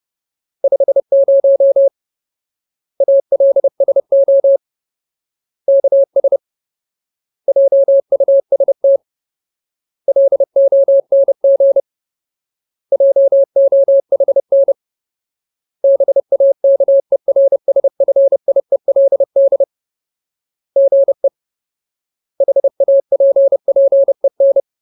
Morse code, heard as a keyed tone, about 30 wpm, 550 Hz. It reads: 50 ALSO KS JUST LONG JOHN BAKERSFIELD GE HAPPEN
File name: Ultimate Word List / 30WPM Ultimate